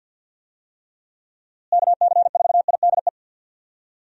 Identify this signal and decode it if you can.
Morse code, heard as a keyed tone, about 52 wpm, 700 Hz. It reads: KC4IDE
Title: USA Call Signs - 6 Characters - Repeated 2x - 50wpm